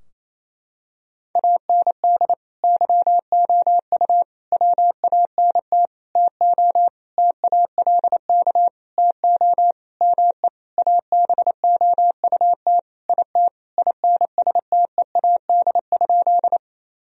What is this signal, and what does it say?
Morse code, heard as a keyed tone, about 28 wpm, 700 Hz. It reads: AND YOU WANT TO TALK TO ME ABOUT IT INSTEAD?
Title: Sentences from Top 1000 Words - Review Part 4 - 28wpm